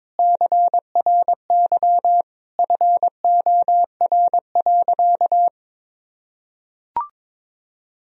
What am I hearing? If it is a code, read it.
Morse code, heard as a keyed tone, about 22 wpm, 700 Hz. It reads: CRY FOR.
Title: Sentences from Top 1000 Words - Review Part 2 - 22wpm